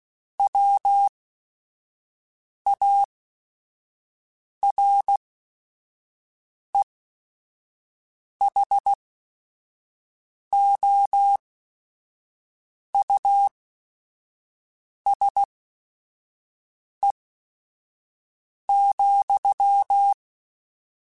Morse code, heard as a keyed tone, about 16 wpm, spaced as for 5 wpm, 800 Hz. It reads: WAREHOUSE,